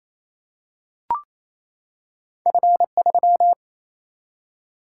Morse code, heard as a keyed tone, about 28 wpm, 700 Hz. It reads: F 3
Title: Single-Letter Single-Number 28wpm